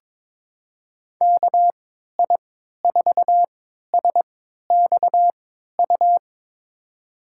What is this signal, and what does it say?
Morse code, heard as a keyed tone, about 21 wpm, spaced as for 12 wpm, 700 Hz. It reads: KI4SXU
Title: USA Call Signs - 6 Characters - 3x Character Spacing - 22wpm